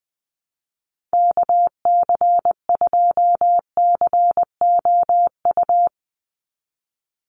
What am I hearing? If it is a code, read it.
Morse code, heard as a keyed tone, about 20 wpm, 700 Hz. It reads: KC2COU
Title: USA Call Signs 20wpm